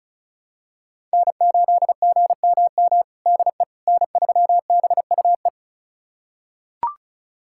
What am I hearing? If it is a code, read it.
Morse code, heard as a keyed tone, about 36 wpm, 700 Hz. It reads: N8GMM DE N3BUE
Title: USA Callsign DE Callsign - 5 Characters - 35wpm